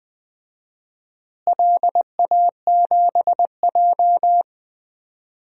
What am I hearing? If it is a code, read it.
Morse code, heard as a keyed tone, about 20 wpm, 700 Hz. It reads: LA7J
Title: DXCC-100 Call Signs - 4 Characters - Repeated 3x - 20wpm